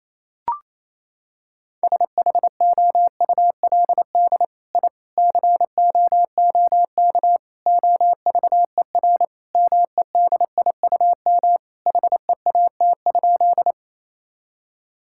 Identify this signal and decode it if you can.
Morse code, heard as a keyed tone, about 27 wpm, 700 Hz. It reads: SHOULD I COOK OVER MEDIUM HEAT?
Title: Sentences from Top 1000 Words - Review Part 6 - 28wpm